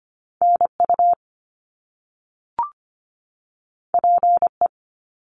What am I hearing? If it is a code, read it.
Morse code, heard as a keyed tone, about 25 wpm, 700 Hz. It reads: NU PE
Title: Canadian Provinces and Territories 25wpm